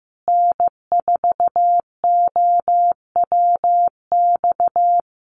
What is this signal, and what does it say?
Morse code, heard as a keyed tone, about 15 wpm, 700 Hz. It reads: N4OWX
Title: USA Call Signs Repeated 3x 15wpm